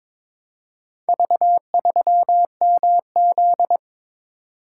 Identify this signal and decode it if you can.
Morse code, heard as a keyed tone, about 22 wpm, 700 Hz. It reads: V3MZ
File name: practice / 22wpm / Four Letter-Number Combinations - 22wpm